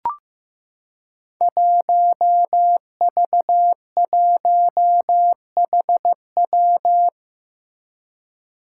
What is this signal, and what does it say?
Morse code, heard as a keyed tone, about 15 wpm, 700 Hz. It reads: 1V1HW
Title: Five Challenging Letter-Number Combinations 15wpm